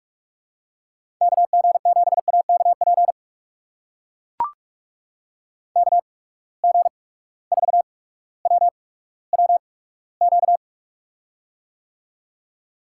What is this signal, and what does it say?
Morse code, heard as a keyed tone, about 44 wpm, spaced as for 19 wpm, 700 Hz. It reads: KG7AKP KG4WWQ
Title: USA Call Signs - 6 Characters - 8x Character Spacing - 45wpm